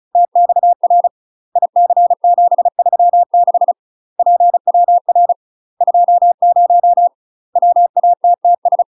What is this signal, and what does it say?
Morse code, heard as a keyed tone, about 36 wpm, 700 Hz. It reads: TXR IC736 PWR 20 WATTS
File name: UK MASTER QSOs TRAINING / 35WPM / UK MASTER QSOs 0032